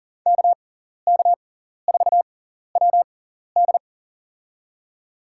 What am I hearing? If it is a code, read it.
Morse code, heard as a keyed tone, about 40 wpm, spaced as for 14 wpm, 700 Hz. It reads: KK4WD
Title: USA Call Signs - 5 Characters - 6x Character Spacing - 40wpm